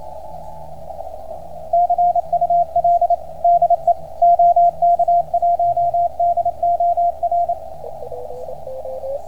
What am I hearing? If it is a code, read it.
Morse code, heard as a keyed tone, about 27 wpm, 690 Hz, with radio noise. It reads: CUL DE OK1DOR